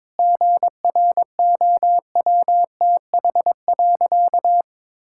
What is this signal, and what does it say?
Morse code, heard as a keyed tone, about 22 wpm, 700 Hz. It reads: GROWTH.
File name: Difficult Sentences with Hints - 22wpm